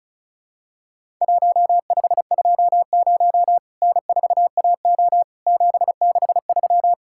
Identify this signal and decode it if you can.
Morse code, heard as a keyed tone, about 34 wpm, 700 Hz. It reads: 1520 N4AO 763
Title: Worked All Europe DX Contest - QTC Exchange - 35wpm